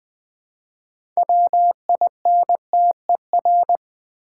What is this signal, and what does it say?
Morse code, heard as a keyed tone, about 20 wpm, 700 Hz. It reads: WINTER
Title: Easy Words with Hints - 20wpm